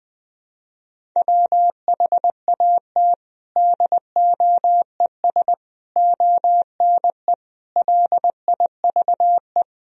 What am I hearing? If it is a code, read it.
Morse code, heard as a keyed tone, about 20 wpm, 700 Hz. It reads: WHAT DOES ONE LIVE
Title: Sentences from Top 200 Words and Abbreviations 20wpm